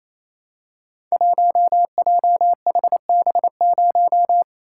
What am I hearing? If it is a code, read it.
Morse code, heard as a keyed tone, about 28 wpm, 700 Hz. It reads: 1JHB0
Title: Code Groups - 5 Letter:Numbers - 28wpm